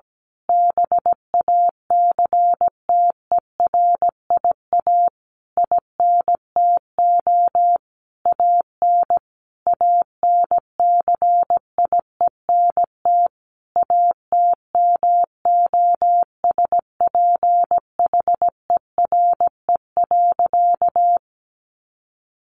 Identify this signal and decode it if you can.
Morse code, heard as a keyed tone, about 17 wpm, 700 Hz. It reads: BACTERIA INTO AN ANCIENT ATMOSPHERE.
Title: A Sound Of Thunder 17wpm